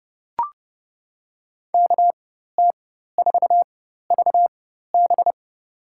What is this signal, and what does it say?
Morse code, heard as a keyed tone, about 32 wpm, spaced as for 14 wpm, 700 Hz. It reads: KT4VB
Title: USA Call Signs - 5 Characters - 4x Character Spacing - 30wpm